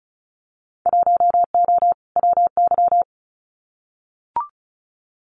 Morse code, heard as a keyed tone, about 36 wpm, 700 Hz. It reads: 1O WY
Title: Winter Field Day Exchanges Encore 35wpm